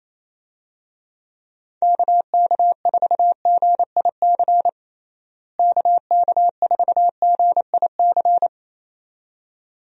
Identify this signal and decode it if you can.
Morse code, heard as a keyed tone, about 28 wpm, 700 Hz. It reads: KK4GIC KK4GIC
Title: USA Call Signs - 6 Characters - Repeated 2x - 28wpm